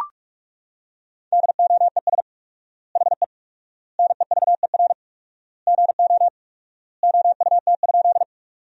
Morse code, heard as a keyed tone, about 41 wpm, 700 Hz. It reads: DOES HE NEVER GO OUT?
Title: 4x Word-Spacing - Sentences from Top 300 Words - 45wpm